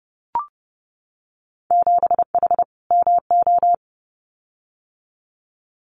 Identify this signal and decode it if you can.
Morse code, heard as a keyed tone, about 30 wpm, 700 Hz. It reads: 7H MO
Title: Winter Field Day Exchanges Encore 30wpm